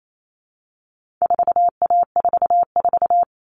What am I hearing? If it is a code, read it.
Morse code, heard as a keyed tone, about 28 wpm, 700 Hz. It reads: 4A44
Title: All Cut Numbers - 4-Digit 28wpm